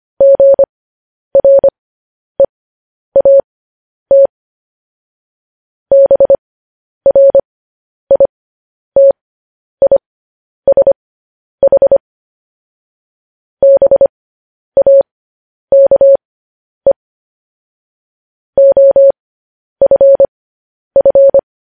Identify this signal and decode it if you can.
Morse code, heard as a keyed tone, about 25 wpm, spaced as for 10 wpm, 550 Hz. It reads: GREAT BRITISH BAKE OFF